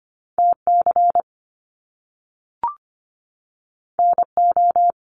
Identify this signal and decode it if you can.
Morse code, heard as a keyed tone, about 25 wpm, 700 Hz. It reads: TC NO
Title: Top Two Letter Combinations in Words 25wpm